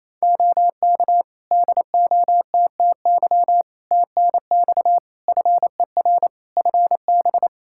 Morse code, heard as a keyed tone, about 27 wpm, 700 Hz. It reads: OK DOTTY TNX FER FB
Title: K1USN SST QSO Practice 28wpm